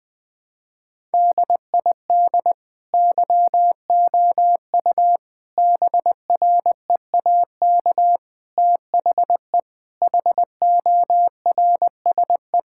Morse code, heard as a keyed tone, about 21 wpm, 700 Hz. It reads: DID YOU BREAK THE HORSE
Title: Sentences from Top 1000 Words - Review Part 6 - 20wpm